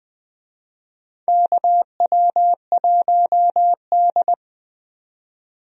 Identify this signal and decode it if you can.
Morse code, heard as a keyed tone, about 20 wpm, 700 Hz. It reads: KW1D
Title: USA Call Signs - 4 Characters - 20wpm